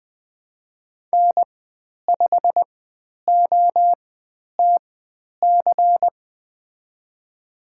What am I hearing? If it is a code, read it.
Morse code, heard as a keyed tone, about 20 wpm, spaced as for 10 wpm, 700 Hz. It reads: N5OTC
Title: 109 Lesson 18g - Letter m - Call Signs 20wpm